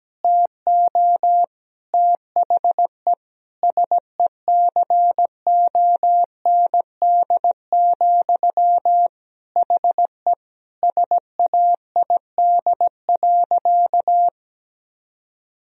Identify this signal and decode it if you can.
Morse code, heard as a keyed tone, about 17 wpm, 700 Hz. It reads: TO THE SECOND, HE SAID.
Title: Sentences from Top 1000 Words - Review Part 1 - 17wpm